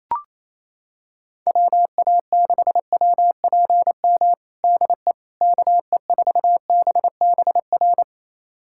Morse code, heard as a keyed tone, about 27 wpm, 700 Hz. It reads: WA6WPM DE KE4BBR
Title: USA Callsign DE Callsign - 6 Characters - 28wpm